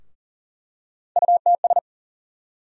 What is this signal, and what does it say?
Morse code, heard as a keyed tone, about 40 wpm, 700 Hz. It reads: UTS
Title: USA Call Sign Postfixes - 3 Letters 40wpm